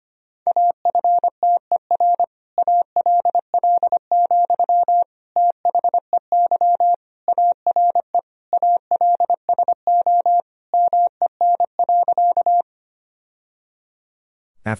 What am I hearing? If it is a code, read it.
Morse code, heard as a keyed tone, about 25 wpm, 700 Hz. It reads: AFTER ALL, THEY ARE ALSO MEN.
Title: Sentences from Top 1000 Words - Review Part 1 - 25wpm